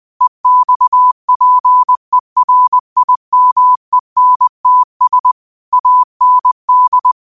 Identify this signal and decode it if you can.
Morse code, heard as a keyed tone, about 20 wpm, 1000 Hz. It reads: EXPERIMENTS AND
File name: qotdfiles / 20wpm